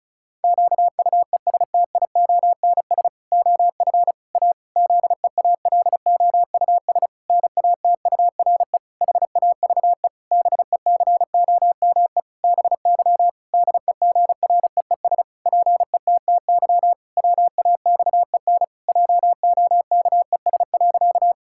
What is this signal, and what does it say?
Morse code, heard as a keyed tone, about 36 wpm, 700 Hz. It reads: QUESTIONS OF A ZEALOUS NATURE HAVE BECOME BY DEGREES PETTY WAXEN JOKES.
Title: Pangram Challenge v2 - 35wpm